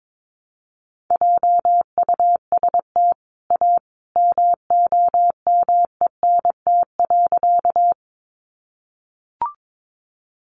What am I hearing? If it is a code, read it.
Morse code, heard as a keyed tone, about 22 wpm, 700 Hz. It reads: JUST A MOMENT.
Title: Sentences from Top 1000 Words 22wpm